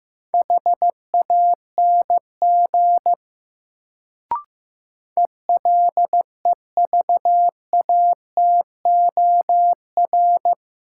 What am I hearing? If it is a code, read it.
Morse code, heard as a keyed tone, about 16 wpm, 700 Hz. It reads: HANG ELEVATOR
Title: Sentences from Top 1600 Words - New Word Drill Alt - 15wpm